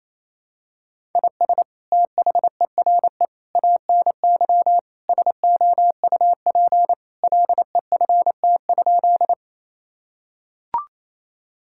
Morse code, heard as a keyed tone, about 28 wpm, 700 Hz. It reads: IS THERE ANY SOUP LEFT?
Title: Sentences from Top 1000 Words - Review Part 6 - 28wpm